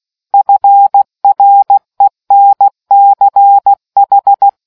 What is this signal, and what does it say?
Morse code, heard as a keyed tone, about 16 wpm, 800 Hz. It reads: FRENCH